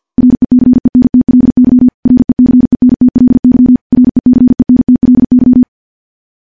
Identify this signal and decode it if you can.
Morse code, heard as a keyed tone, about 50 wpm, 250 Hz. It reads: RECENTLY RECENTLY RECENTLY
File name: morse / morse-training-50WPM